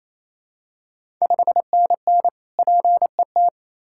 Morse code, heard as a keyed tone, about 28 wpm, 700 Hz. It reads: 5NN PET
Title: Ontario QSO Party Exchange 28wpm